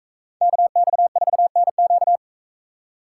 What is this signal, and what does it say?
Morse code, heard as a keyed tone, about 41 wpm, 700 Hz. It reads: KX4NQ